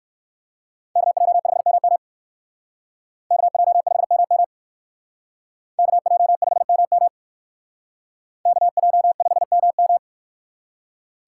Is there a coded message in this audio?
Yes: Morse code from 1.0 to 10.0 s, heard as a keyed tone, about 64 wpm, 700 Hz. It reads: KJ5MM KJ5MM KJ5MM KJ5MM